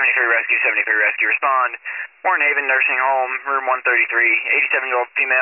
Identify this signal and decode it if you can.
Emergency Alert System (SAME) tones, heard as harsh buzzing bursts, with speech. It is heard from 0.9 to 4.2 s.